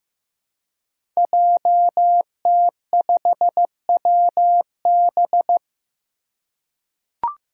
Morse code, heard as a keyed tone, about 15 wpm, 700 Hz. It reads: JT5WB
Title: Five Challenging Letter-Number Combinations 15wpm